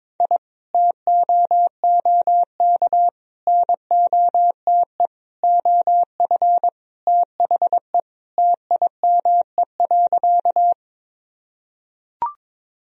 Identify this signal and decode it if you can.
Morse code, heard as a keyed tone, about 21 wpm, 700 Hz. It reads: I TOOK NOTE OF THE TIME.
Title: Sentences from Top 1000 Words - Review Part 2 - 22wpm